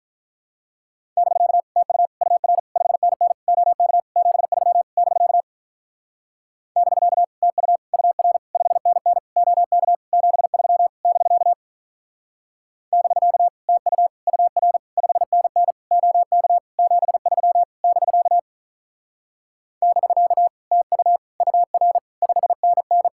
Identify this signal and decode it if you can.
Morse code, heard as a keyed tone, about 47 wpm, 700 Hz. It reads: <BK> TU UR 5NN OK 73 <BK> <BK> TU UR 5NN OK 73 <BK> <BK> TU UR 5NN OK 73 <BK> <BK> TU UR 5NN